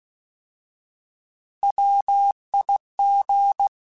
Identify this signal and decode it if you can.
Morse code, heard as a keyed tone, about 16 wpm, 800 Hz. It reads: WIG
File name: Alphabetical / Alphabetical mp3s / Ws @15